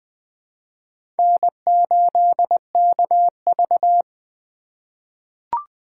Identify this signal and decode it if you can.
Morse code, heard as a keyed tone, about 20 wpm, 700 Hz. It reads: N8KV
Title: USA Call Signs - 4 Characters - 6x Character Spacing - 20wpm